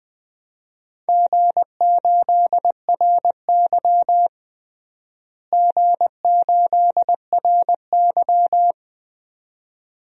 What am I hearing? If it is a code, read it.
Morse code, heard as a keyed tone, about 20 wpm, 700 Hz. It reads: G8RY G8RY
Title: DXCC-100 Call Signs - 4 Characters - Repeated 2x - 20wpm